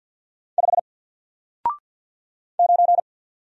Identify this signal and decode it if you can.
Morse code, heard as a keyed tone, about 50 wpm, 700 Hz. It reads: 5 9
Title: Single Letter-Number Alt 50wpm